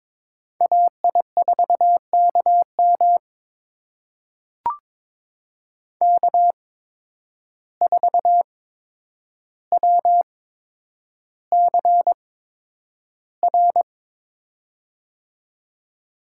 Morse code, heard as a keyed tone, about 22 wpm, spaced as for 9 wpm, 700 Hz. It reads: AI4KM K4WCR
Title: USA Call Signs - 5 Characters - 8x Character Spacing - 22wpm